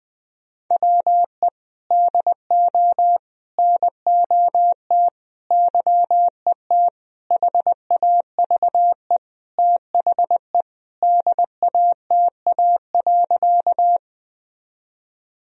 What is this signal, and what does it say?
Morse code, heard as a keyed tone, about 20 wpm, 700 Hz. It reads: WE DO NOT YET HAVE THE DATA.
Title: Sentences from Top 1000 Words - Review Part 3 - 20wpm